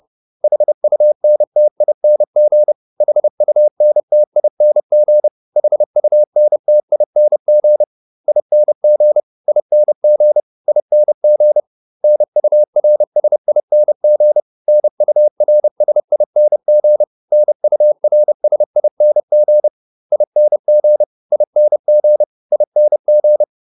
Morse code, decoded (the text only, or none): HUNTING HUNTING HUNTING ING ING ING NURSING NURSING NURSING ING ING ING